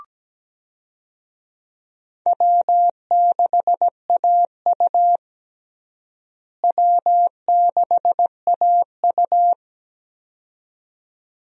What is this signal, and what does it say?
Morse code, heard as a keyed tone, about 17 wpm, 700 Hz. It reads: W6AU W6AU
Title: USA Call Signs - 4 Characters - Repeated 2x - 17wpm